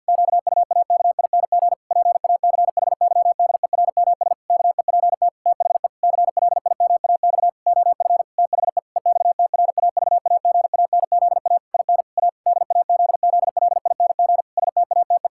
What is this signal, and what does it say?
Morse code, heard as a keyed tone, about 52 wpm, 700 Hz. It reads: QUAKING PACHYDERMS KEPT THE CLIMAX OF THE EXTRAVAGANZA IN A DAZZLING STATE
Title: Pangram Challenge v2 - 50wpm